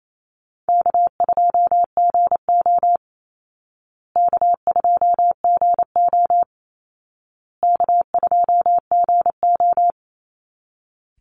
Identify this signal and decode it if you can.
Morse code, heard as a keyed tone, about 28 wpm, 700 Hz. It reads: K2GO K2GO K2GO
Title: Call Signs Repeated 2x - 28wpm